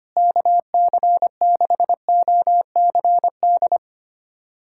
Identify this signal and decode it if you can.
Morse code, heard as a keyed tone, about 25 wpm, 700 Hz. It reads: KC6OCD
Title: USA Callsign DE Callsign - 6 Characters - 25wpm